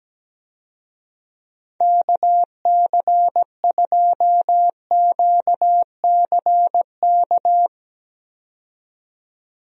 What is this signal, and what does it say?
Morse code, heard as a keyed tone, about 17 wpm, 700 Hz. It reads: KC2QCK
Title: USA Call Signs - 6 Characters - 17wpm